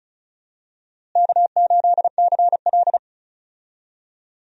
Morse code, heard as a keyed tone, about 36 wpm, 700 Hz. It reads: K8CL
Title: USA Call Signs - 4 Characters - Repeated 2x - 35wpm